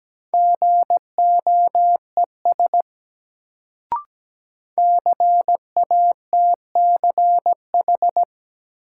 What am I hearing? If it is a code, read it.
Morse code, heard as a keyed tone, about 17 wpm, 700 Hz. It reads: GOES CATCH
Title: Sentences from Top 1500 Words - New Word Drill Alt - 17wpm